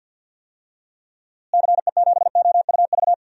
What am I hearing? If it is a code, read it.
Morse code, heard as a keyed tone, about 51 wpm, 700 Hz. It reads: CE7OUV
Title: DXCC-100 Call Signs - 6 Characters - Repeated 3x - 50wpm